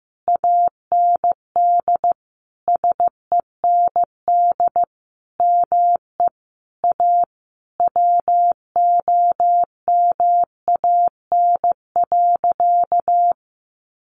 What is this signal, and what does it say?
Morse code, heard as a keyed tone, about 15 wpm, 700 Hz. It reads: AND SEND ME A WOMAN.
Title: Sentences from Top 1500 Words 15wpm